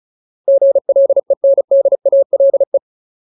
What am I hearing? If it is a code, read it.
Morse code, heard as a keyed tone, about 34 wpm, 550 Hz. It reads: GLENDALE
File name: Ultimate Word List / 35WPM Ultimate